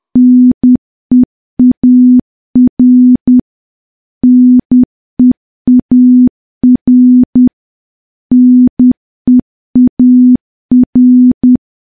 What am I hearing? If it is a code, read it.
Morse code, heard as a keyed tone, about 10 wpm, 250 Hz. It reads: NEAR NEAR NEAR
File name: morse / morse-training-10WPM